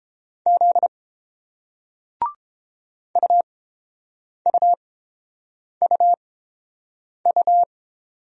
Morse code, heard as a keyed tone, about 30 wpm, spaced as for 7 wpm, 690 Hz. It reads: Z UUUU